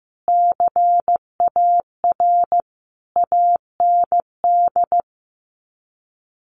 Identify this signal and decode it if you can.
Morse code, heard as a keyed tone, about 15 wpm, 700 Hz. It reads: CAR AND
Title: Sets of 5 Words Encore 15wpm